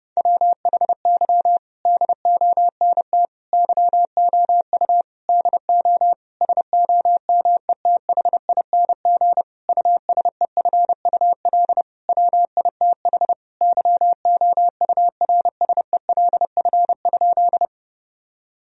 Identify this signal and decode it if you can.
Morse code, heard as a keyed tone, about 32 wpm, 700 Hz. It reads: WHY DONT YOU DO SOMETHING USEFUL WITH YOURSELF?
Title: Sentences from Top 1000 Words - Review Part 4 - 30wpm